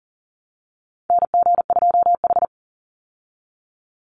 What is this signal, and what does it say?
Morse code, heard as a keyed tone, about 40 wpm, 700 Hz. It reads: NG2H
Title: Call Signs Repeated 2x - 40wpm